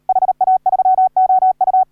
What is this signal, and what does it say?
Morse code, heard as a keyed tone, about 38 wpm, 750 Hz. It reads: HA3OU